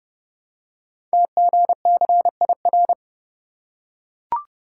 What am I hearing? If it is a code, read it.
Morse code, heard as a keyed tone, about 30 wpm, 690 Hz. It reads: TGCIR
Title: Code Groups - 5 Letters - 30wpm